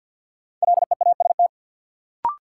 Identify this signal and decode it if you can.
Morse code, heard as a keyed tone, about 50 wpm, 700 Hz. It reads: LEAST
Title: Sentences from Top 700 Words - New Word Drill - 50wpm